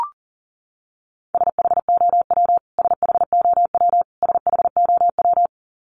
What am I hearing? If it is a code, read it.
Morse code, heard as a keyed tone, about 40 wpm, 700 Hz. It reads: SHOW SHOW SHOW
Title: Top 200 Words Repeated 40wpm